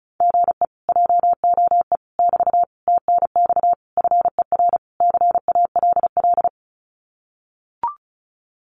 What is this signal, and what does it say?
Morse code, heard as a keyed tone, about 34 wpm, 700 Hz. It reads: GE JOE <BT> TNX FER CALL
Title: Common QSO Phrases Encore 35wpm